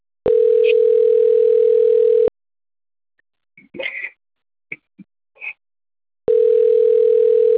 Ringback tone (North America), 440 + 480 Hz.